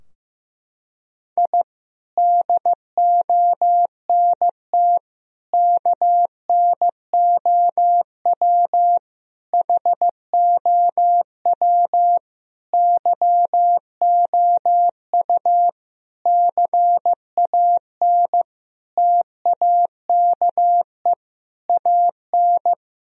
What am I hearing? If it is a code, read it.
Morse code, heard as a keyed tone, about 15 wpm, 700 Hz. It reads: I DONT KNOW HOW YOU CAN TAKE AN
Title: Sentences from Top 1000 Words - Review Part 2 - 15wpm